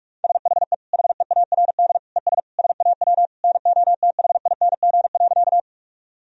Morse code, heard as a keyed tone, about 43 wpm, 700 Hz. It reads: SHE HEARD ES SAW NOTHING.
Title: Sentences from Top 500 Words and Abbreviations 45wpm